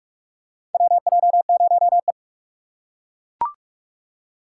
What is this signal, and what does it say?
Morse code, heard as a keyed tone, about 42 wpm, 700 Hz. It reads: WJ0E
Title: USA Call Signs - 4 Characters - 3x Character Spacing - 45wpm